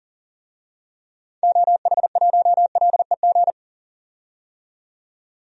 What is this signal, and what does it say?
Morse code, heard as a keyed tone, about 40 wpm, 700 Hz. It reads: OH1LEG
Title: DXCC-100 Call Signs - 6 Characters - 40wpm